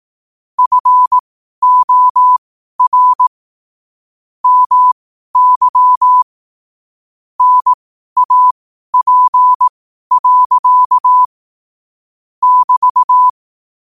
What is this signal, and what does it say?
Morse code, heard as a keyed tone, about 18 wpm, spaced as for 12 wpm, 1000 Hz. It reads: FOR MY NAP. <BT>
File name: qotdfiles / 12wpm